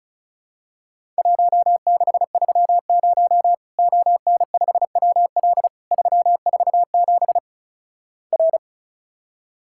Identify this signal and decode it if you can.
Morse code, heard as a keyed tone, about 36 wpm, 700 Hz. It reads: 1630 ON5WL 347 R
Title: Worked All Europe DX Contest - QTC Exchange - 35wpm